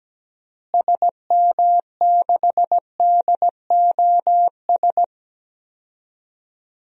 Morse code, heard as a keyed tone, about 17 wpm, 700 Hz. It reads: SM6DOS
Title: DXCC-100 Call Signs - 6 Characters - Repeated 2x - 17wpm